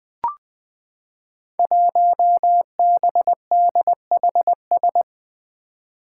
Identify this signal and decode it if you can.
Morse code, heard as a keyed tone, about 21 wpm, 700 Hz. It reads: 1BDHS
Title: Five Challenging Letter-Number Combinations 20wpm